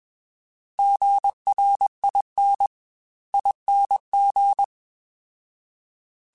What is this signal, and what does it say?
Morse code, heard as a keyed tone, about 21 wpm, spaced as for 18 wpm, 800 Hz. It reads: GRINING
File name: Alphabetical / Alphabetical mp3s / Gs @20